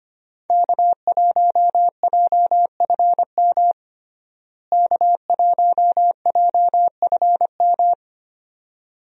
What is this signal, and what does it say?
Morse code, heard as a keyed tone, about 25 wpm, 700 Hz. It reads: K1JFM K1JFM
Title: USA Call Signs - 5 Characters - Repeated 2x - 25wpm